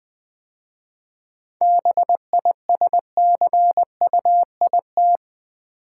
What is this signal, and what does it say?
Morse code, heard as a keyed tone, about 21 wpm, 690 Hz. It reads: BISCUIT
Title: 7 Letter Words - First 3 Letters Given - 20wpm